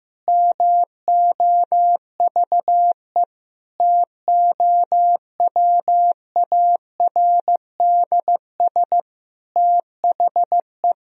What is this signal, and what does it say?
Morse code, heard as a keyed tone, about 15 wpm, 700 Hz. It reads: MOVE TOWARDS THE
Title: Sentences from Top 1000 Words - Review Part 2 - 15wpm